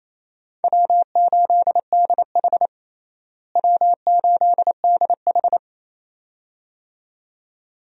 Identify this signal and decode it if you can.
Morse code, heard as a keyed tone, about 28 wpm, 700 Hz. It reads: W8DH W8DH
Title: USA Call Signs - 4 Characters - Repeated 2x - 28wpm